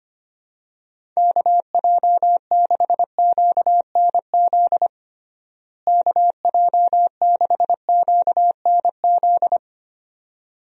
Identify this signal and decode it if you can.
Morse code, heard as a keyed tone, about 25 wpm, 700 Hz. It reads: KJ6QNZ KJ6QNZ
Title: USA Call Signs - 6 Characters - Repeated 2x - 25wpm